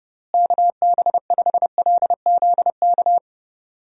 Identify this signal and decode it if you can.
Morse code, heard as a keyed tone, about 30 wpm, 700 Hz. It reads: KB5LZK
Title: USA Call Signs - 6 Characters - Repeated 2x - 30wpm